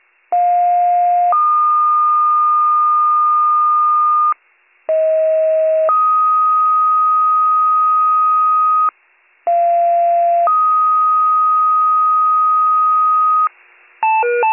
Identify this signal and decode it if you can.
Two-tone page, heard as a one-second tone, then a three-second tone, about 710 then 1220 Hz, about 640 then 1250 Hz, about 690 then 1250 Hz. It is heard from 0.3 to 13.5 s.